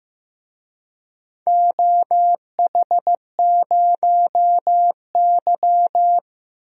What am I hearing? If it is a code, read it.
Morse code, heard as a keyed tone, about 15 wpm, 700 Hz. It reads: OH0Y
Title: DXCC-100 Call Signs - 4 Characters - 15wpm